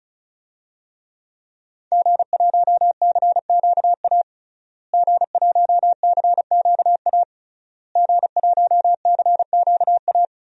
Morse code, heard as a keyed tone, about 36 wpm, 700 Hz. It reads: G1CQA G1CQA G1CQA
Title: DXCC-100 Call Signs - 5 Characters - Repeated 3x - 35wpm